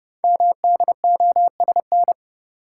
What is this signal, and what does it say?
Morse code, heard as a keyed tone, about 29 wpm, 700 Hz. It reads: MDOSN
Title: Code Groups - 5 Letter:Numbers - 30wpm